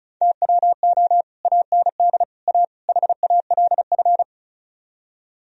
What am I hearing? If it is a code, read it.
Morse code, heard as a keyed tone, about 45 wpm, 700 Hz. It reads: TWO AND A HALF